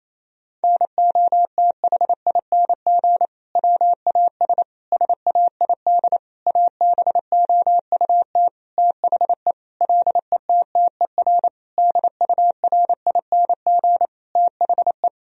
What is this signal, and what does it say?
Morse code, heard as a keyed tone, about 28 wpm, 700 Hz. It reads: NOTHING WAS SAID ABOUT THE LETTER DURING THE
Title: Sentences from Top 1000 Words - Review Part 2 - 28wpm